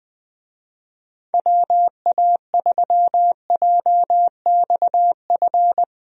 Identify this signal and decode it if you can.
Morse code, heard as a keyed tone, about 21 wpm, 700 Hz. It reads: WA3JXF
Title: USA Call Signs - 6 Characters - 20wpm